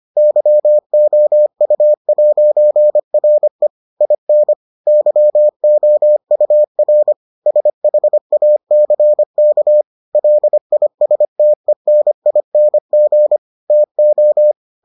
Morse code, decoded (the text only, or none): YOU'RE IN YOUR SHACK LISTENING TO